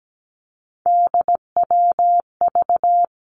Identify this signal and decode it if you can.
Morse code, heard as a keyed tone, about 17 wpm, 700 Hz. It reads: DWV
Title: Three Challenging Letter-Number Combinations 17wpm